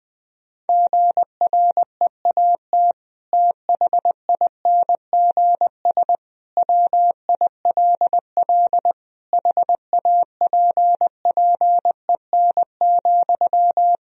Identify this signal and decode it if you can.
Morse code, heard as a keyed tone, about 20 wpm, 700 Hz. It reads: GREAT THINGS WILL HAPPEN,
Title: Sentences from Top 1000 Words - Review Part 4 - 20wpm